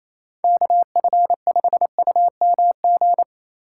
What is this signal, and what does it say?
Morse code, heard as a keyed tone, about 28 wpm, 700 Hz. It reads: KF5UMG
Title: USA Call Signs - 6 Characters - 3x Character Spacing - 28wpm